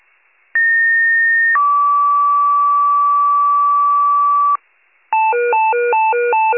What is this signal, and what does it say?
Two-tone page, heard as a one-second tone, then a three-second tone, about 1850 then 1150 Hz. It is heard from 0.5 to 4.6 s.